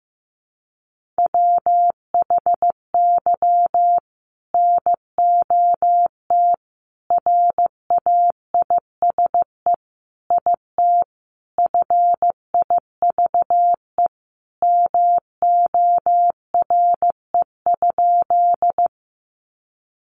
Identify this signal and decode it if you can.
Morse code, heard as a keyed tone, about 15 wpm, 700 Hz. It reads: WHY NOT RAISE IT FIVE MORE?
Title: Sentences from Top 1400 Words 15wpm